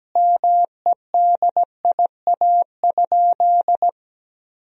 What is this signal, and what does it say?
Morse code, heard as a keyed tone, about 17 wpm, 700 Hz. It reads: MEDIA?
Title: Sentences from Top 1000 Words - Review Part 4 - 17wpm